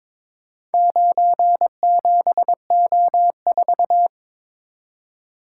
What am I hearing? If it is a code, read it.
Morse code, heard as a keyed tone, about 22 wpm, 700 Hz. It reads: 97O4
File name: practice / 22wpm / Four Letter-Number Combinations - 22wpm